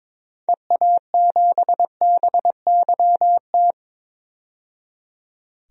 Morse code, heard as a keyed tone, about 22 wpm, 700 Hz. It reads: EA7BYT
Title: DXCC-100 Call Signs - 6 Characters - 22wpm